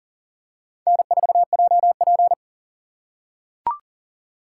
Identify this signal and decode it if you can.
Morse code, heard as a keyed tone, about 38 wpm, 700 Hz. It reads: N4JP
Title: USA Call Signs - 6x Character Spacing - 40wpm